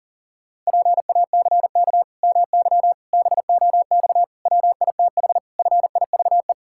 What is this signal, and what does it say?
Morse code, heard as a keyed tone, about 40 wpm, 700 Hz. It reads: PACK MY BOX WITH FIVE
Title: Pangram Challenge v2 - 40wpm